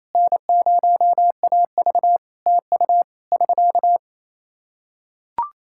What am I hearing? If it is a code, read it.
Morse code, heard as a keyed tone, about 28 wpm, 700 Hz. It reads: N0AV TU <SK>
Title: K1USN SST QSO Practice 28wpm